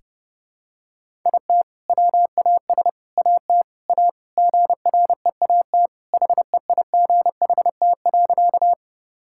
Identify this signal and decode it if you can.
Morse code, heard as a keyed tone, about 32 wpm, 700 Hz. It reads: IT WAS AT A GREAT HEIGHT.
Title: Sentences from Top 1600 Words 30wpm